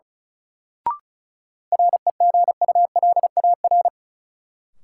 Morse code, heard as a keyed tone, about 36 wpm, 700 Hz. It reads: REGULAR
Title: Sentences from Top 900 Words - New Word Drill - 35wpm